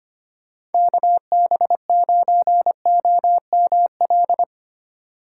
Morse code, heard as a keyed tone, about 30 wpm, 700 Hz. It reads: KB9OML
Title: Speed Racing - USA 2x3 Call Signs 25wpm